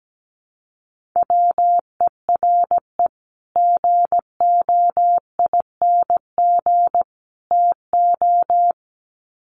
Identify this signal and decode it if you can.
Morse code, heard as a keyed tone, about 17 wpm, 700 Hz. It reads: WERE GOING TO
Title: Sets of 3 Words 17wpm